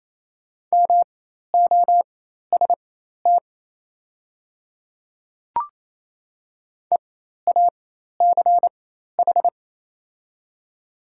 Morse code, heard as a keyed tone, about 28 wpm, spaced as for 13 wpm, 700 Hz. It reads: MOST EACH
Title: 4x Character Spacing - Top 100 Words - 28wpm